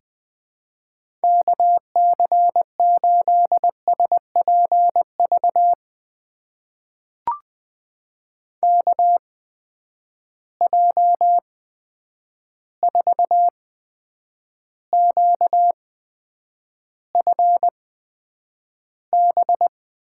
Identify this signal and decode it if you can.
Morse code, heard as a keyed tone, about 20 wpm, spaced as for 9 wpm, 700 Hz. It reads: KC8SPV KJ4QFB
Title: USA Call Signs - 6 Characters - 8x Character Spacing - 20wpm